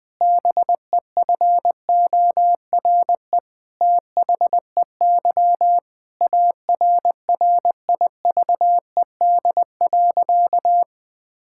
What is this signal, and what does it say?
Morse code, heard as a keyed tone, about 20 wpm, 700 Hz. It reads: BEFORE THEY ARRIVED.
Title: Sentences with 7-Word Phrases - 20wpm